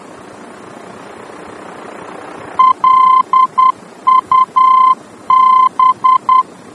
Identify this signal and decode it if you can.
Morse code, heard as a keyed tone, about 10 wpm, 1040 Hz. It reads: LUB
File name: germany / mx / LUB-D-VOR